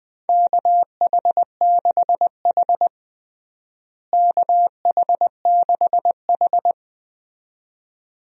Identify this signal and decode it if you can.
Morse code, heard as a keyed tone, about 21 wpm, 700 Hz. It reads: KH6H KH6H
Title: DXCC-100 Call Signs - 4 Characters - Repeated 2x - 20wpm